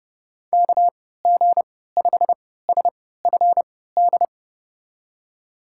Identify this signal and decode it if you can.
Morse code, heard as a keyed tone, about 32 wpm, spaced as for 17 wpm, 700 Hz. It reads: KG5SFD
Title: USA Call Signs - 6 Characters - 3x Character Spacing - 30wpm